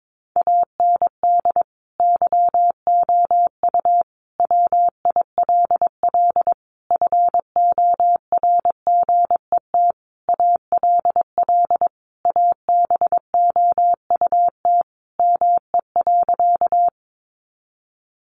Morse code, heard as a keyed tone, about 22 wpm, 700 Hz. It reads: AND YOU WILL FORGET ALL ABOUT ME.
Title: Sentences from Top 1200 Words 22wpm